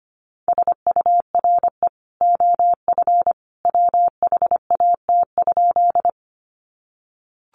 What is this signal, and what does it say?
Morse code, heard as a keyed tone, about 25 wpm, 700 Hz. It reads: SURE OF WHAT?
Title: Sentences from Top 300 Words 25wpm